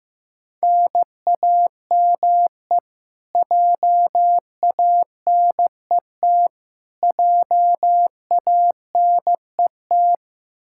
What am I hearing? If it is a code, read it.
Morse code, heard as a keyed tone, about 15 wpm, 700 Hz. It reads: NAME JANET JANET
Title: Practice QSO - Exchange 1 - Beginning - 15wpm